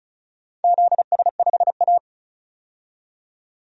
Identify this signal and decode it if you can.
Morse code, heard as a keyed tone, about 34 wpm, 700 Hz. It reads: ZS5A
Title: DXCC-100 Call Signs - 4 Characters - 35wpm